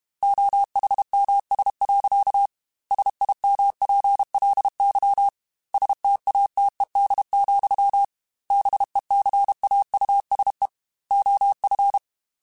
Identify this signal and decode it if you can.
Morse code, heard as a keyed tone, about 32 wpm, 800 Hz. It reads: OHMS. SIMPLY STATED, BECAUSE OF
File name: Technical Subjects / Technical Subjects-mp3s / XC @30